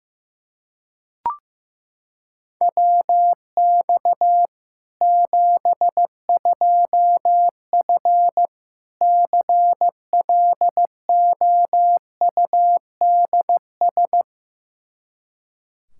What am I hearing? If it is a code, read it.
Morse code, heard as a keyed tone, about 15 wpm, 700 Hz. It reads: WX 72F CLOUDS
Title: K1USN SST QSO Practice 15wpm